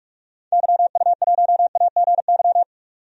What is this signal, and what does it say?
Morse code, heard as a keyed tone, about 46 wpm, 700 Hz. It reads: YU1AGY
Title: DXCC-100 Call Signs - 6 Characters - 45wpm